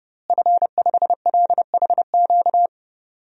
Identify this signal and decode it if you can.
Morse code, heard as a keyed tone, about 30 wpm, 700 Hz. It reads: F5LHQ
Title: DXCC-100 Call Signs - 5 Characters - Repeated 3x - 30wpm